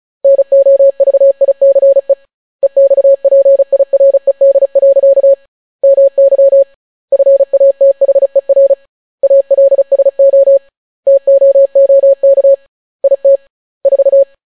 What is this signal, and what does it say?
Morse code, heard as a keyed tone, about 36 wpm, 550 Hz. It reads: NOVICE EXPIRED. MY FATHER ALSO TOOK IT 4